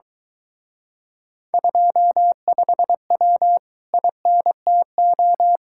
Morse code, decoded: 25W INTO

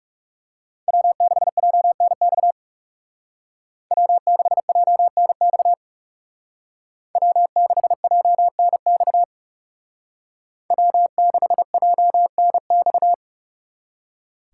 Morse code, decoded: W6JNX W6JNX W6JNX W6JNX

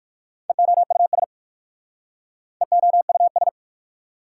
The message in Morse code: EOUS EOUS

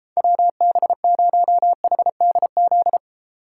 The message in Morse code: WB0HDZ

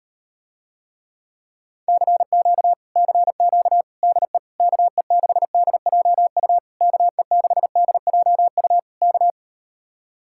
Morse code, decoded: CQ CQ DE KE6DJU KE6DJU K

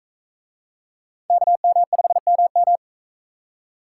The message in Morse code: KM5MM

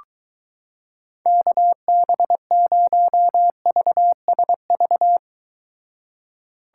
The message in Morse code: KB0VSV

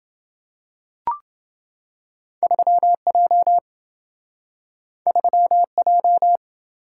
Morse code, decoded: 3J 3J

